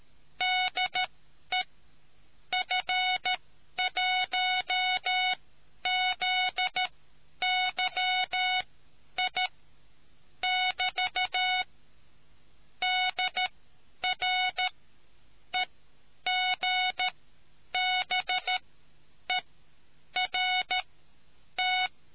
Morse code, heard as a keyed tone, about 13 wpm, spaced as for 9 wpm, 740 Hz. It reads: DE F1ZYI <BT> DR EGBERT